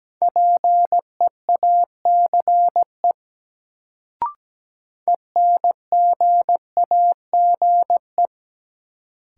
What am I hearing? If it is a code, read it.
Morse code, heard as a keyed tone, about 17 wpm, 700 Hz. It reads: PEACE ENGAGE
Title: Sentences from Top 1500 Words - New Word Drill - 17wpm